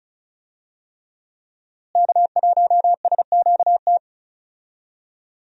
Morse code, heard as a keyed tone, about 36 wpm, 700 Hz. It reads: K1SQT
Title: USA Call Signs - 5 Characters - Repeated 3x - 35wpm